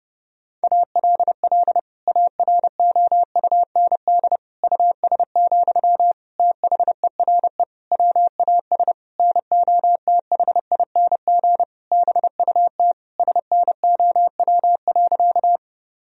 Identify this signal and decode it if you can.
Morse code, heard as a keyed tone, about 30 wpm, 700 Hz. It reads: ALL AROUND US, THERE WAS NOTHING BUT SNOW.
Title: Sentences from Top 1000 Words - Review Part 3 - 30wpm